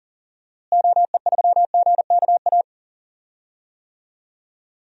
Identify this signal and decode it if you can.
Morse code, heard as a keyed tone, about 40 wpm, 700 Hz. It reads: OE3GKA
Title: DXCC-100 Call Signs - 6 Characters - Repeated 2x - 40wpm